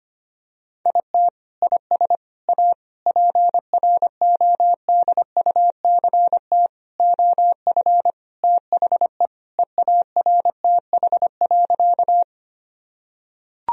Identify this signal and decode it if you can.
Morse code, heard as a keyed tone, about 25 wpm, 700 Hz. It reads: IT IS A PRODUCT OF THE EARTH.
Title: Sentences from Top 1000 Words - Review Part 2 - 25wpm